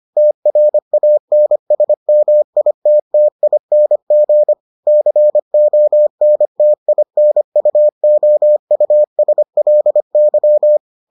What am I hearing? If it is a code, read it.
Morse code, heard as a keyed tone, about 25 wpm, 600 Hz. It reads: TRANSMITTING CONTINUOUSLY